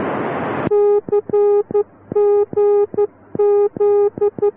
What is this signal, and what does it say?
Morse code, heard as a keyed tone, about 12 wpm, 410 Hz. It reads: CGZ